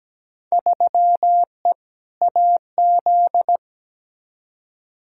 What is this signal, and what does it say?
Morse code, heard as a keyed tone, about 17 wpm, 700 Hz. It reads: 3E AZ
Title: Field Day Exchanges - 17wpm